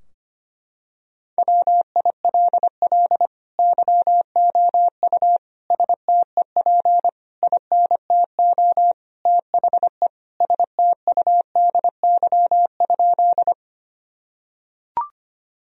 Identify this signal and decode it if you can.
Morse code, heard as a keyed tone, about 25 wpm, 700 Hz. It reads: WILL YOU STEP INTO THE STUDY?
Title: Sentences from Top 1000 Words - Review Part 2 - 25wpm